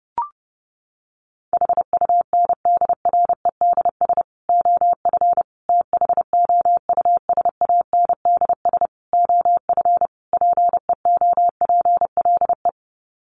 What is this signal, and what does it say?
Morse code, heard as a keyed tone, about 28 wpm, 700 Hz. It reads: HUNDREDS OF THOUSANDS OF PEOPLE
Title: Sets of 5 Words Encore2 30wpm